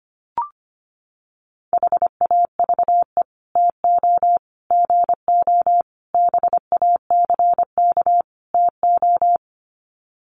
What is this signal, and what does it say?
Morse code, heard as a keyed tone, about 25 wpm, 700 Hz. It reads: HAVE TO GO BACK TO
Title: Sets of 5 Words Encore2 25wpm